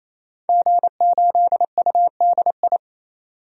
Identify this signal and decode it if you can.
Morse code, heard as a keyed tone, about 28 wpm, 700 Hz. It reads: G8UDI
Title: DXCC-100 Call Signs - 5 Characters - 28wpm